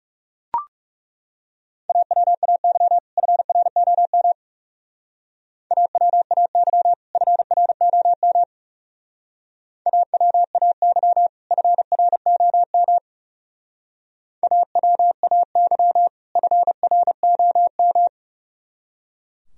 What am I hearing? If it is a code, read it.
Morse code, heard as a keyed tone, about 38 wpm, 700 Hz. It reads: AWAY FROM AWAY FROM AWAY FROM AWAY FROM